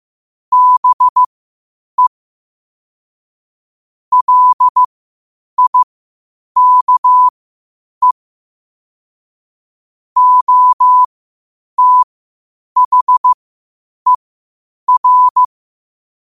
Morse code, heard as a keyed tone, about 15 wpm, spaced as for 8 wpm, 1000 Hz. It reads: BE LIKE OTHER